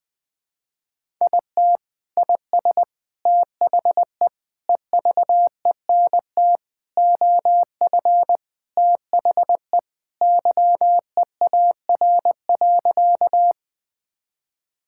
Morse code, heard as a keyed tone, about 21 wpm, 700 Hz. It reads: IT IS THE EVENT OF THE YEAR.
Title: Sentences with 3-Word Phrases - 20wpm